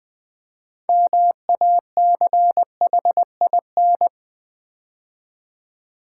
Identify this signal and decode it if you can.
Morse code, heard as a keyed tone, about 20 wpm, 700 Hz. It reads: MACHIN
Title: Build A Word - 3 Letters At A Time - 9-Letter Words - 20wpm